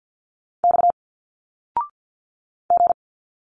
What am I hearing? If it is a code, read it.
Morse code, heard as a keyed tone, about 51 wpm, 700 Hz. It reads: X G
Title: Single Letters 50wpm